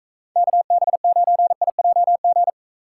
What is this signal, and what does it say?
Morse code, heard as a keyed tone, about 38 wpm, 700 Hz. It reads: KB9IJG